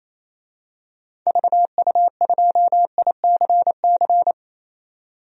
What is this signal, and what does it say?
Morse code, heard as a keyed tone, about 28 wpm, 700 Hz. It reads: VU2ICC
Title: DXCC-100 Call Signs - 6 Characters - 28wpm